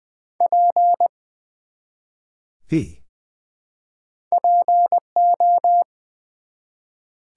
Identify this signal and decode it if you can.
Morse code, heard as a keyed tone, about 20 wpm, 700 Hz. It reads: P PO
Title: Build A Word - 1 Letter At A Time - 4-Letter Words - 20wpm